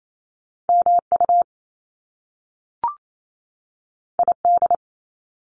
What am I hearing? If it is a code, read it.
Morse code, heard as a keyed tone, about 28 wpm, 700 Hz. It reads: MU ID
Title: Top Two Letter Combinations in Words 28wpm